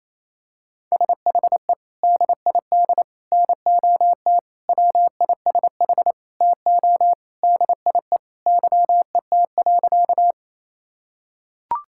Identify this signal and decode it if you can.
Morse code, heard as a keyed tone, about 27 wpm, 700 Hz. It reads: SHE DID NOT WISH TO DIE YET.
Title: Sentences from Top 1600 Words 28wpm